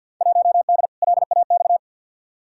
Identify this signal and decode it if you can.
Morse code, heard as a keyed tone, about 48 wpm, 700 Hz. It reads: 1D LAX
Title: Field Day Exchanges - 50wpm